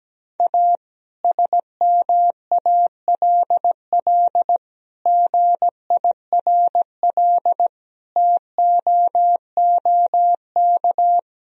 Morse code, heard as a keyed tone, about 17 wpm, 700 Hz. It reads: A SMALL GIRL TOOK
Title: Sentences from Top 1000 Words - Review Part 1 - 17wpm